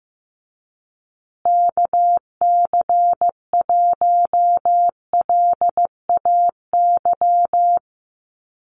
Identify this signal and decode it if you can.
Morse code, heard as a keyed tone, about 15 wpm, 700 Hz. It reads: KC1LAY
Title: USA Call Signs Repeated 2x 15wpm